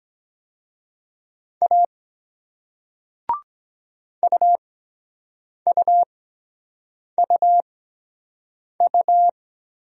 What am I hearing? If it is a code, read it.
Morse code, heard as a keyed tone, about 23 wpm, spaced as for 7 wpm, 700 Hz. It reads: A UUUU